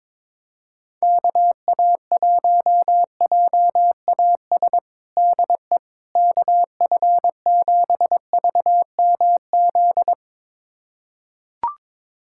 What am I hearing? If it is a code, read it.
Morse code, heard as a keyed tone, about 22 wpm, 700 Hz. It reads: KA1JAS DE KF7VMZ
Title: USA Callsign DE Callsign - 6 Characters - 22wpm